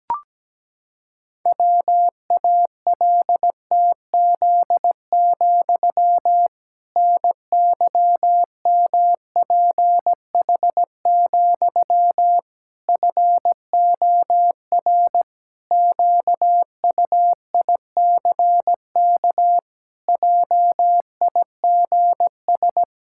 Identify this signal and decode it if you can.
Morse code, heard as a keyed tone, about 17 wpm, 700 Hz. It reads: WALTZ, NYMPH, FOR QUICK JIGS
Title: Pangram Challenge v2 - 17wpm